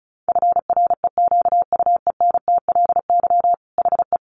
Morse code, heard as a keyed tone, about 34 wpm, 700 Hz. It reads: FREQUENTLY HE
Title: Sentences from Top 1300 Words 35wpm